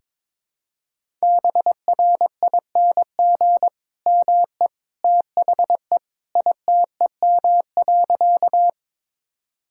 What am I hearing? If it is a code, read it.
Morse code, heard as a keyed tone, about 22 wpm, 700 Hz. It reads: BRING ME THE ITEM.
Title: Sentences from Top 1000 Words - Review Part 6 - 22wpm